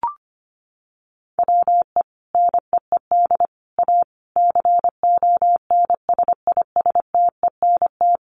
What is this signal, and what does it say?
Morse code, heard as a keyed tone, about 25 wpm, 700 Hz. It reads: WE NEED A CONSISTENT
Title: Sentences from Top 1500 Words 25wpm